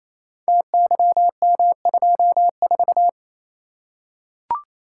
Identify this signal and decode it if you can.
Morse code, heard as a keyed tone, about 29 wpm, 700 Hz. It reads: TYM24
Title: Code Groups - 5 Letter:Numbers - 28wpm